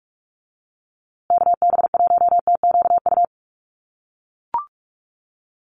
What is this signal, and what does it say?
Morse code, heard as a keyed tone, about 44 wpm, 700 Hz. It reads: KB1TQU
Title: USA Call Signs Repeated 2x 45wpm